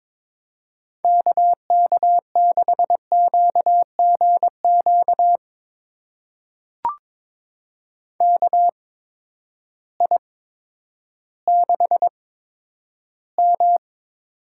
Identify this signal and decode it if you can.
Morse code, heard as a keyed tone, about 22 wpm, spaced as for 11 wpm, 700 Hz. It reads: KK6QGQ KI6M
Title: USA Call Signs - 6 Characters - 8x Character Spacing - 22wpm